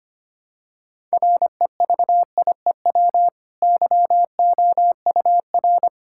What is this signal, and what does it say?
Morse code, heard as a keyed tone, about 25 wpm, 700 Hz. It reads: REVIEW YOUR
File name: Easy Sentences with Hints - 25wpm